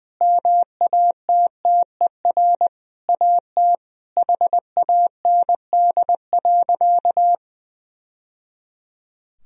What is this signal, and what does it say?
Morse code, heard as a keyed tone, about 21 wpm, 700 Hz. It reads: MATTER AT HAND.
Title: Sentences from Top 1000 Words - Review Part 3 - 20wpm